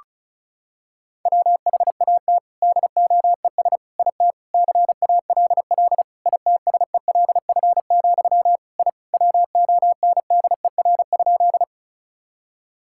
Morse code, heard as a keyed tone, about 36 wpm, 700 Hz. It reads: WHAT DOES IT CALL ITSELF, I WONDER?
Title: Sentences from Top 1000 Words - Review Part 5 - 35wpm